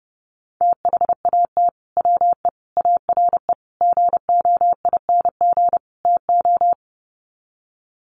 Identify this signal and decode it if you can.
Morse code, heard as a keyed tone, about 28 wpm, 700 Hz. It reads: THAT WE ARE GOING TO
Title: Sets of 5 Words Encore 30wpm